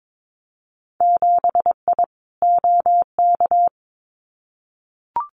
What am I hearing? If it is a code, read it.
Morse code, heard as a keyed tone, about 22 wpm, 700 Hz. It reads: 7I OK
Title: Winter Field Day Exchanges Encore 22wpm